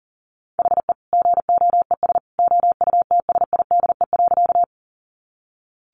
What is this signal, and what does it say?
Morse code, heard as a keyed tone, about 40 wpm, 700 Hz. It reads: HE GOES OUTSIDE.
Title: Easy Phrases 40wpm